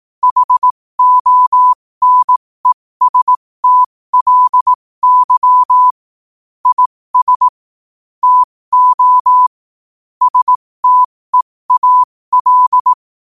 Morse code, heard as a keyed tone, about 18 wpm, spaced as for 15 wpm, 1000 Hz. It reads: HONESTLY IS TO STEAL
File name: qotdfiles / 15wpm